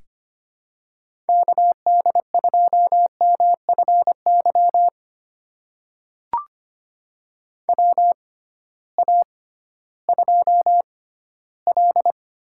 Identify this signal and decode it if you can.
Morse code, heard as a keyed tone, about 25 wpm, spaced as for 15 wpm, 700 Hz. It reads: KD2MFY WA2L
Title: USA Call Signs - 6 Characters - 6x Character Spacing - 25wpm